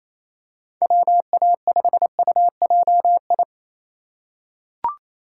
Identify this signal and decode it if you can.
Morse code, heard as a keyed tone, about 27 wpm, 700 Hz. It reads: WA5UJI
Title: USA Call Signs - 3x Character Spacing - 28wpm